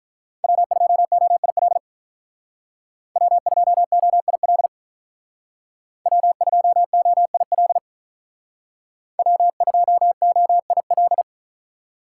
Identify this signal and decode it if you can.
Morse code, heard as a keyed tone, about 47 wpm, 700 Hz. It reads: W2OIL W2OIL W2OIL W2OIL